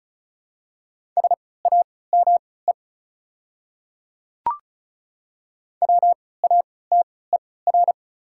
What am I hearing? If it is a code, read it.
Morse code, heard as a keyed tone, about 36 wpm, spaced as for 20 wpm, 700 Hz. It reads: SAME WATER
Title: 3x Character Spacing - Top 300 Words - 35wpm